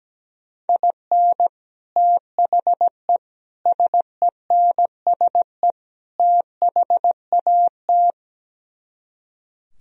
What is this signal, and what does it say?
Morse code, heard as a keyed tone, about 17 wpm, 700 Hz. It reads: IN THE SENSE THAT
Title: Speed-Racing - Sets of 4 Words 17wpm